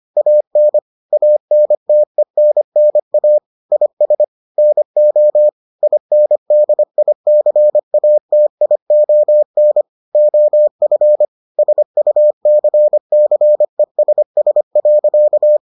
Morse code, heard as a keyed tone, about 25 wpm, 600 Hz. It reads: AN ANTENNA IS NO INDICATION OF SUCCESS.